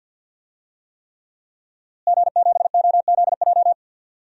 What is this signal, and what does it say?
Morse code, heard as a keyed tone, about 52 wpm, 700 Hz. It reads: G7OZJ
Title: DXCC-100 Call Signs - 5 Characters - Repeated 3x - 50wpm